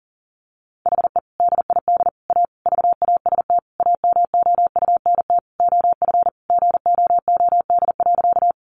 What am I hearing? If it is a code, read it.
Morse code, heard as a keyed tone, about 40 wpm, 700 Hz. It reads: HE DID A VAST AMOUNT OF GOOD.
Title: Sentences from Top 1300 Words 40wpm